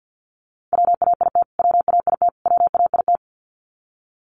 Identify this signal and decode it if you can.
Morse code, heard as a keyed tone, about 50 wpm, 700 Hz. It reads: WAIT WAIT WAIT
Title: Top 500 Words Repeated 50wpm